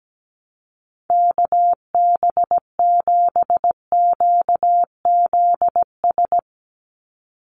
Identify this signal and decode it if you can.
Morse code, heard as a keyed tone, about 17 wpm, 700 Hz. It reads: KB7QZS
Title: USA Call Signs 17wpm